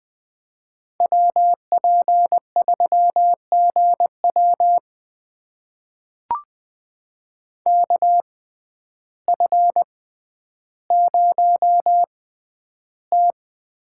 Morse code, heard as a keyed tone, about 20 wpm, spaced as for 11 wpm, 700 Hz. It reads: WP3GW KF0T
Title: USA Call Signs - 5 Characters - 6x Character Spacing - 20wpm